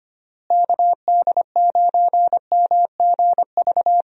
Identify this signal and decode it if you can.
Morse code, heard as a keyed tone, about 25 wpm, 700 Hz. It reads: KD9MGV
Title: USA Call Signs - 6 Characters - Repeated 2x - 25wpm